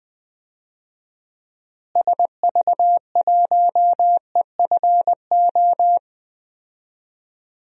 Morse code, heard as a keyed tone, about 20 wpm, 700 Hz. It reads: SV1EFO
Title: DXCC-100 Call Signs - 6 Characters - Repeated 3x - 20wpm